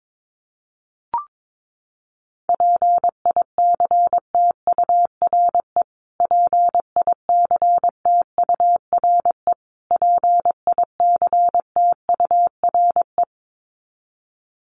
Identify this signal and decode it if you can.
Morse code, heard as a keyed tone, about 22 wpm, 700 Hz. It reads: PICTURE PICTURE PICTURE
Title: Top 500 Words Repeated 22wpm